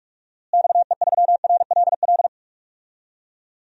Morse code, heard as a keyed tone, about 43 wpm, 700 Hz. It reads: XE3RLL
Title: DXCC-100 Call Signs - 6 Characters - Repeated 3x - 45wpm